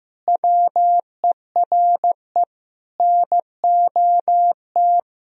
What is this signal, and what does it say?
Morse code, heard as a keyed tone, about 15 wpm, 700 Hz. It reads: WERE NOT
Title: Sentences from Top 1000 Words - Review Part 3 - 15wpm